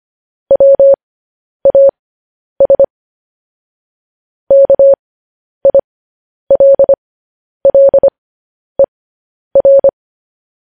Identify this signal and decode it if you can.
Morse code, heard as a keyed tone, about 25 wpm, spaced as for 10 wpm, 550 Hz. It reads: WAS KILLER